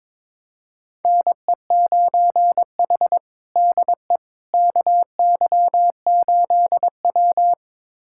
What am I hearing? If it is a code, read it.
Morse code, heard as a keyed tone, about 22 wpm, 700 Hz. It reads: NE9H DE KY8W
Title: USA Callsign DE Callsign - 4 Characters - 22wpm